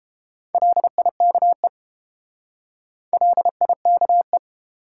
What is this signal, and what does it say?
Morse code, heard as a keyed tone, about 30 wpm, 700 Hz. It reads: LIKE LIKE